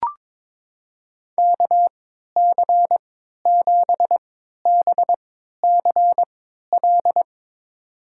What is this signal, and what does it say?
Morse code, heard as a keyed tone, about 23 wpm, spaced as for 13 wpm, 700 Hz. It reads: KC7BCL
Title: USA Call Signs - 3x Character Spacing - 22wpm